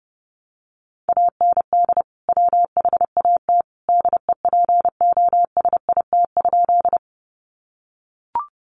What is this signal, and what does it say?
Morse code, heard as a keyed tone, about 30 wpm, 700 Hz. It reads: AND WHAT DEPOSIT?
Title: Sentences from Top 1300 Words 30wpm